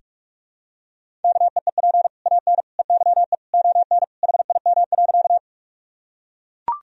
Morse code, heard as a keyed tone, about 42 wpm, 700 Hz. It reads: KEEP AN EYE ON HIM.
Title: Sentences from Top 1000 Words - Review Part 1 - 45wpm